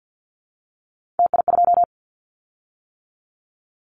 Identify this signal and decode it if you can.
Morse code, heard as a keyed tone, about 49 wpm, 700 Hz. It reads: TI2
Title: Callsign Prefixes for Top 100 DXCC - 50wpm